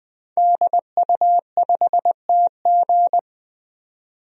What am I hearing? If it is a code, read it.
Morse code, heard as a keyed tone, about 20 wpm, 700 Hz. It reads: DU5TG
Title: DXCC-100 Call Signs - 5 Characters - 20wpm